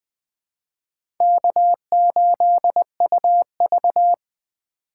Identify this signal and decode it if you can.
Morse code, heard as a keyed tone, about 20 wpm, 700 Hz. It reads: K8UV
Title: USA Call Signs - 4 Characters - 2x Character Spacing - 20wpm